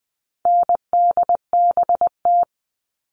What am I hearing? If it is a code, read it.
Morse code, heard as a keyed tone, about 20 wpm, 700 Hz. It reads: NDBT
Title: All Cut Numbers - 4-Digit 20wpm